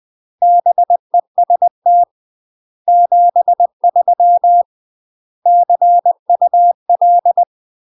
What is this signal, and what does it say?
Morse code, heard as a keyed tone, about 20 wpm, 700 Hz. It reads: BEST 73 CUL
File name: UK MASTER QSOs TRAINING / 20WPM / UK MASTER QSOs 0012